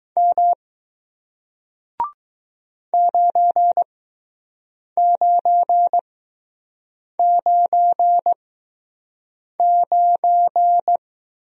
Morse code, heard as a keyed tone, about 20 wpm, spaced as for 6 wpm, 700 Hz. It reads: M 9999